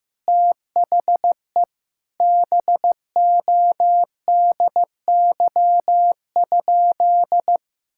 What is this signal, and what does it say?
Morse code, heard as a keyed tone, about 15 wpm, 700 Hz. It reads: THE BODY?
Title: Sentences from Top 1000 Words - Review Part 6 - 15wpm